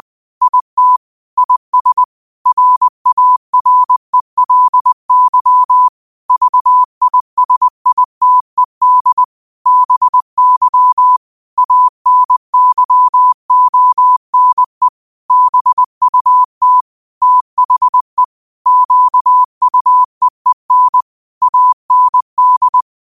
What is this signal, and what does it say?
Morse code, heard as a keyed tone, about 20 wpm, 1000 Hz. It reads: IT IS RARELY VISITED BY ANYONE BUT THE QUEEN AND
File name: qotdfiles / 20wpm